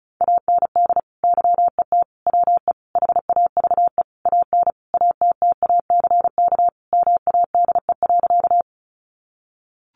Morse code, heard as a keyed tone, about 34 wpm, 700 Hz. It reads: AND YET WE HAVE AN ATTACK MADE.
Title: Sentences from Top 1500 Words 35wpm